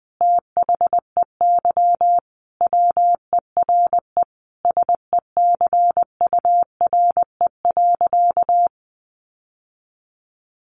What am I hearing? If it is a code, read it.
Morse code, heard as a keyed tone, about 20 wpm, 700 Hz. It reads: THEY WERE SECURE.
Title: Sentences from Top 1200 Words 20wpm